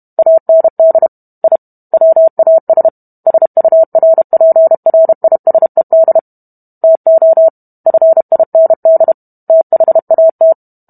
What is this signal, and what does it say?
Morse code, heard as a keyed tone, about 32 wpm, 650 Hz. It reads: AND I WAS SURPRISED TO FIND THAT